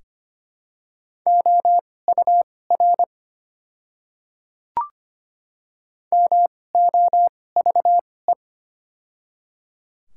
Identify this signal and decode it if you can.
Morse code, heard as a keyed tone, about 25 wpm, spaced as for 18 wpm, 700 Hz. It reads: OUR MOVE
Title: 2x Character Spacing - Top 200 Words - 25wpm